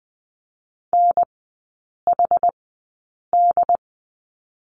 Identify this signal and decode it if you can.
Morse code, heard as a keyed tone, about 20 wpm, spaced as for 8 wpm, 700 Hz. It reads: NHD
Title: 081 Lesson 14e - Number 2 - Three Characters 20wpm